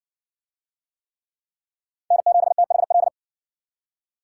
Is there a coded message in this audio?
Yes: Morse code from 2.1 to 3.1 s, heard as a keyed tone, about 54 wpm, 700 Hz. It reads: N7THL